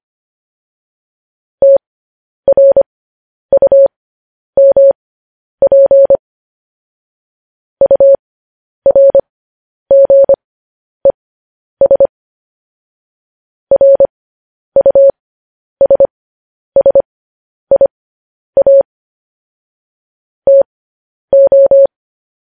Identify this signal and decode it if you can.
Morse code, heard as a keyed tone, about 25 wpm, spaced as for 10 wpm, 550 Hz. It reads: TRUMP URGES RUSSIA TO
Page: news headlines in morse code at 10 wpm